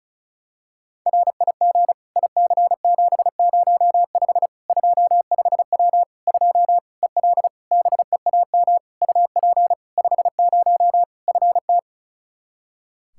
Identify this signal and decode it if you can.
Morse code, heard as a keyed tone, about 36 wpm, 700 Hz. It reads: RIG IC705 25W 2 EL BEAM UP 50 FT
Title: Practice QSO - Exchange 2 - Info - 35wpm